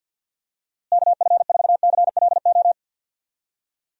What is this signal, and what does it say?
Morse code, heard as a keyed tone, about 47 wpm, 700 Hz. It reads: KF4CLO
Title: USA Call Signs - 6 Characters - Repeated 3x - 50wpm